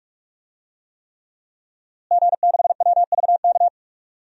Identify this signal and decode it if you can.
Morse code, heard as a keyed tone, about 47 wpm, 700 Hz. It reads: G6WVK
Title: DXCC-100 Call Signs - 5 Characters - Repeated 3x - 45wpm